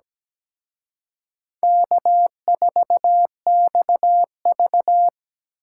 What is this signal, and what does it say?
Morse code, heard as a keyed tone, about 20 wpm, 700 Hz. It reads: K4XV
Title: Speed Racing - USA 1x2 Call Signs 17wpm